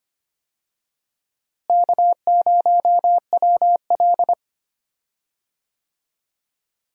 Morse code, heard as a keyed tone, about 25 wpm, 700 Hz. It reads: K0WL
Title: USA Call Signs - 4 Characters - Repeated 3x - 25wpm